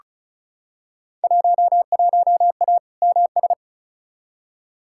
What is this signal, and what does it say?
Morse code, heard as a keyed tone, about 35 wpm, 700 Hz. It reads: 11A MS
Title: Field Day Exchanges Encore - 35wpm